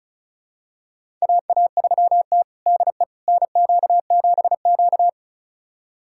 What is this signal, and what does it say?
Morse code, heard as a keyed tone, about 36 wpm, 700 Hz. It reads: AA3T DE NQ7Q
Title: USA Callsign DE Callsign - 4 Characters - 35wpm